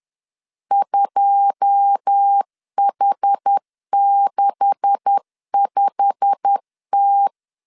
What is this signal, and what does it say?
Morse code, heard as a keyed tone, about 10 wpm, 800 Hz. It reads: 2H65T
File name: LEZIONI / 9 286 10wpm 2735